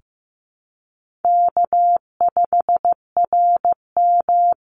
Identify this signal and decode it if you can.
Morse code, heard as a keyed tone, about 15 wpm, 700 Hz. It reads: K5RM
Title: Easy Call Signs Repeated 2x - 15wpm